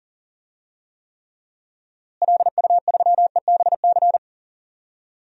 Morse code, heard as a keyed tone, about 40 wpm, 700 Hz. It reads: LU3EBC
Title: DXCC-100 Call Signs - 6 Characters - Repeated 3x - 40wpm